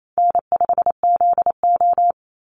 Morse code, heard as a keyed tone, about 28 wpm, 700 Hz. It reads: N5ZO
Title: Easy Call Signs Repeated 2x - 28wpm